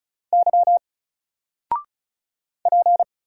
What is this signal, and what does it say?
Morse code, heard as a keyed tone, about 35 wpm, 700 Hz. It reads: Y P
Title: Single Letter-Number 35wpm